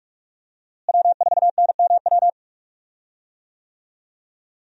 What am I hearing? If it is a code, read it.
Morse code, heard as a keyed tone, about 43 wpm, 700 Hz. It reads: W4NMW
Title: USA Call Signs - 5 Characters - Repeated 3x - 45wpm